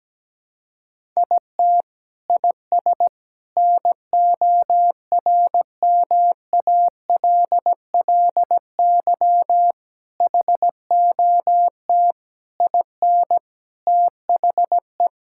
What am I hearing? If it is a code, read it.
Morse code, heard as a keyed tone, about 17 wpm, 700 Hz. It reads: IT IS NORMALLY HOT IN THE
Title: Sentences from Top 1000 Words - Review Part 6 - 17wpm